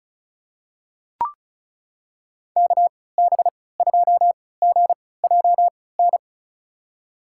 Morse code, heard as a keyed tone, about 36 wpm, spaced as for 20 wpm, 700 Hz. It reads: KB2GJN
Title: USA Call Signs - 3x Character Spacing - 35wpm